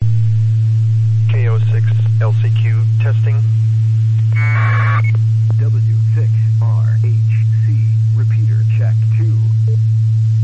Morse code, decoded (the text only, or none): none